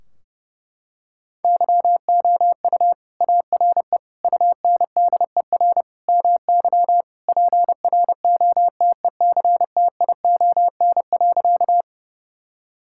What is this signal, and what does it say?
Morse code, heard as a keyed tone, about 32 wpm, 700 Hz. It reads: YOU ARE UNDER MY PROTECTION.
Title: Sentences from Top 1600 Words 30wpm